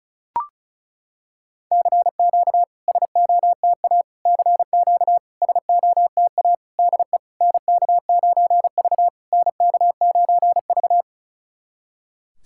Morse code, heard as a keyed tone, about 36 wpm, 700 Hz. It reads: CQ SOTA CQ SOTA DE NK9V NK9V
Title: SOTA QSO Practice 35wpm